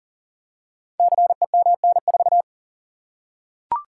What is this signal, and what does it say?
Morse code, heard as a keyed tone, about 41 wpm, 690 Hz. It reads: CEMN4
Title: Code Groups - 5 Letter:Numbers - 40wpm